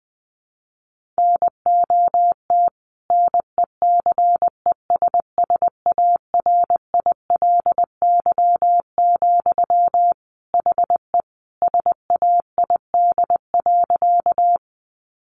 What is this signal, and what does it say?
Morse code, heard as a keyed tone, about 20 wpm, 700 Hz. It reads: NOT NECESSARILY, HE SAID.
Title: Sentences from Top 1400 Words 20wpm